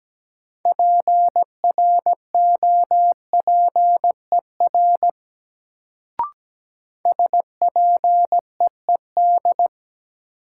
Morse code, heard as a keyed tone, about 17 wpm, 700 Hz. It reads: PROPER SPEED
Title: Sentences from Top 800 Words - New Word Drill - 17wpm